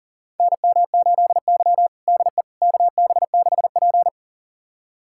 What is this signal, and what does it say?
Morse code, heard as a keyed tone, about 40 wpm, 700 Hz. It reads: NM8Y DE KB6P
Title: USA Callsign DE Callsign - 4 Characters - 40wpm